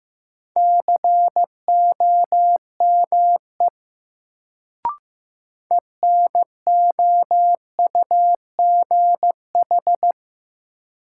Morse code, heard as a keyed tone, about 15 wpm, 700 Hz. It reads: COME ENOUGH
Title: Sets of 2 Words - New Word Drill - 15wpm